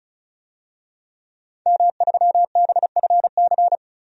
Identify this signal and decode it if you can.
Morse code, heard as a keyed tone, about 35 wpm, 700 Hz. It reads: M3BFC
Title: DXCC-100 Call Signs - 5 Characters - 35wpm